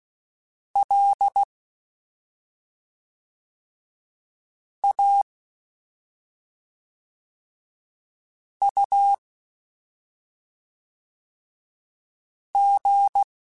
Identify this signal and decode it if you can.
Morse code, heard as a keyed tone, about 16 wpm, 800 Hz. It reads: L A U G
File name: Special INGs / Special INGs-mp3s / SlowToFaster ing @5